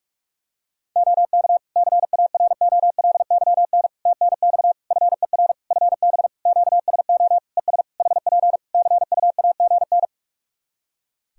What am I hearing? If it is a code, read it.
Morse code, heard as a keyed tone, about 46 wpm, 700 Hz. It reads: OK CAROLYN TNX FER FB QSO ES HP CUAGN
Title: Practice QSO - Exchange 4 - Ending - 45wpm